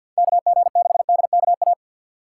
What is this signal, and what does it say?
Morse code, heard as a keyed tone, about 53 wpm, 700 Hz. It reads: KG6DKA
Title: USA Call Signs - 6 Characters - 8x Character Spacing - 50wpm